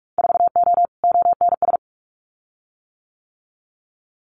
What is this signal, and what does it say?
Morse code, heard as a keyed tone, about 46 wpm, 700 Hz. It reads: 4O ONS
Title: Winter Field Day Exchanges 45wpm